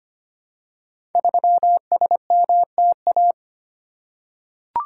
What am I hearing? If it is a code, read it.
Morse code, heard as a keyed tone, about 25 wpm, 700 Hz. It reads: 3SMTA
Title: Code Groups - 5 Letter:Numbers - 25wpm